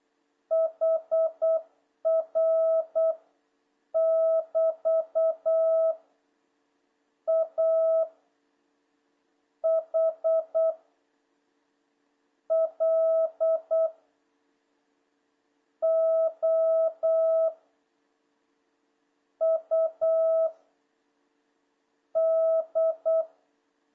Morse code, heard as a keyed tone, about 8 wpm, 1300 Hz, with noise. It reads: HR <BT> A H L O U D
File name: jh8cbh / 2026cw